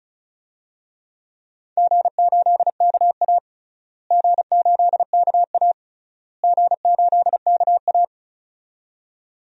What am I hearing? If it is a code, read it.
Morse code, heard as a keyed tone, about 36 wpm, 700 Hz. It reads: G8KA G8KA G8KA
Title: DXCC-100 Call Signs - 4 Characters - Repeated 3x - 35wpm